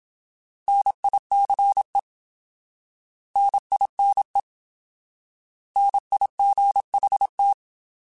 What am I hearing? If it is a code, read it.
Morse code, heard as a keyed tone, about 27 wpm, 800 Hz. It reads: NICE NINE NIGHT
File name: Alphabetical / Alphabetical mp3s / Ns @25